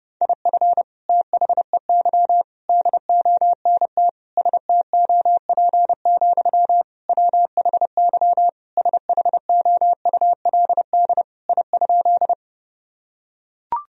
Morse code, heard as a keyed tone, about 30 wpm, 700 Hz. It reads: IF THEY DONT STOP, WHY SHOULD I?
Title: Sentences from Top 1000 Words - Review Part 1 - 30wpm